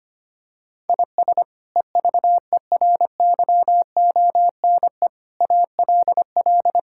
Morse code, heard as a keyed tone, about 25 wpm, 700 Hz. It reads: IS EVERYONE ALL
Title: Sentences from Top 1000 Words - Review Part 5 - 25wpm